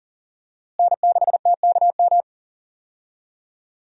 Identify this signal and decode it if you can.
Morse code, heard as a keyed tone, about 39 wpm, 700 Hz. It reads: N6TKM
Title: USA Call Signs - 5 Characters - Repeated 2x - 40wpm